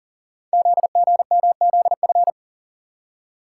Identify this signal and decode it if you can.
Morse code, heard as a keyed tone, about 42 wpm, 700 Hz. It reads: ZGMZF
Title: Code Groups - 5 Letters - 40wpm